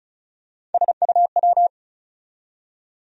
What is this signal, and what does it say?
Morse code, heard as a keyed tone, about 36 wpm, 700 Hz. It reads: SUW
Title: USA Call Sign Postfixes - 3 Letters 35wpm